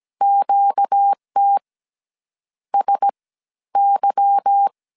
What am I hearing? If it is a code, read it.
Morse code, heard as a keyed tone, about 17 wpm, 800 Hz. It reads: QT S Y
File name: CW / DET / 1 9 1500C S A 16wpm 2828